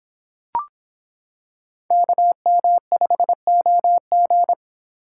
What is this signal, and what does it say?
Morse code, heard as a keyed tone, about 23 wpm, 700 Hz. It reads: KM5OG